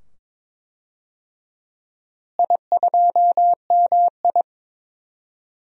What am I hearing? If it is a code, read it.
Morse code, heard as a keyed tone, about 22 wpm, 700 Hz. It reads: I2MI
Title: DXCC-100 Call Signs - 4 Characters - Repeated 2x - 22wpm